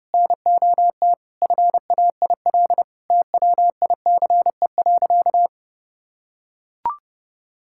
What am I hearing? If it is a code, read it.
Morse code, heard as a keyed tone, about 30 wpm, 700 Hz. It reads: NOT FAIL TWICE.
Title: Sentences from Top 1600 Words 30wpm